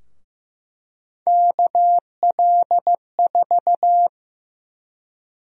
Morse code, heard as a keyed tone, about 15 wpm, 700 Hz. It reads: KL4
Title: USA Call Sign Prefixes - 15wpm